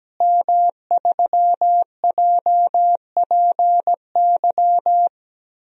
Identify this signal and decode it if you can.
Morse code, heard as a keyed tone, about 17 wpm, 700 Hz. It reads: M3JPY
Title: DXCC-100 Call Signs - 5 Characters - Repeated 3x - 17wpm